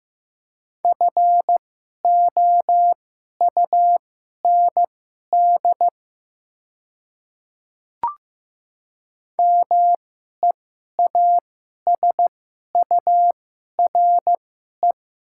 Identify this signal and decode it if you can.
Morse code, heard as a keyed tone, about 16 wpm, spaced as for 11 wpm, 700 Hz. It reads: FOUND MEASURE
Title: 2x Character Spacing - Top 500 Words - 15wpm